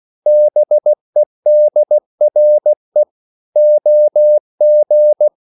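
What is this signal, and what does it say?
Morse code, heard as a keyed tone, about 16 wpm, 600 Hz. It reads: BEDRE OG
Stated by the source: Slow (QRS) Morse code at 16 wpm, from CW med Gnister.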